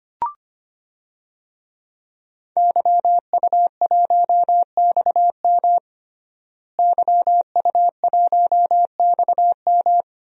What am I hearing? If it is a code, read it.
Morse code, heard as a keyed tone, about 25 wpm, 700 Hz. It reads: YU1XM YU1XM
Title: DXCC-100 Call Signs - 5 Characters - Repeated 3x - 25wpm